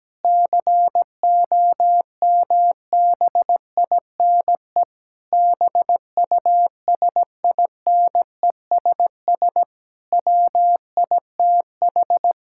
Morse code, heard as a keyed tone, about 17 wpm, 700 Hz. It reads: COMBINE BUSINESS WITH
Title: Sentences from Top 1600 Words 17wpm